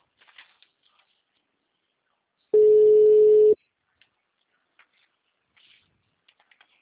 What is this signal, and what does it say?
Ringback tone (Europe, 425 Hz).
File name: out- -2019- - - .13466